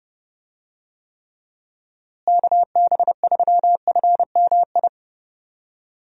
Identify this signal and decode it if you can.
Morse code, heard as a keyed tone, about 30 wpm, 700 Hz. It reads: KB3FMI
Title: USA Call Signs - 6 Characters - Repeated 2x - 30wpm